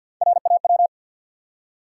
Morse code, heard as a keyed tone, about 54 wpm, 700 Hz. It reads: RAW
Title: Three Letter Words - 50wpm